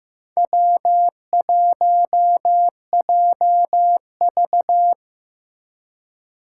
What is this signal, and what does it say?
Morse code, heard as a keyed tone, about 15 wpm, 700 Hz. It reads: W1JV
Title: USA Call Signs - 4 Characters - 3x Character Spacing - 15wpm